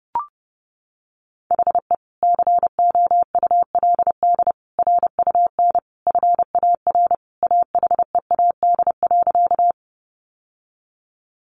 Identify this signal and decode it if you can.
Morse code, heard as a keyed tone, about 30 wpm, 700 Hz. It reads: HE COULD RUN FAR AHEAD.
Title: Sentences from Top 1100 Words 30wpm